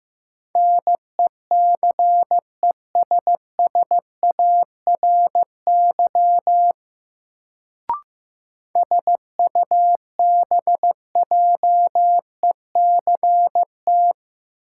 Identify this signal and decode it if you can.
Morse code, heard as a keyed tone, about 15 wpm, 700 Hz. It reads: NECESSARY SUBJECT
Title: Sentences from Top 800 Words - New Word Drill - 15wpm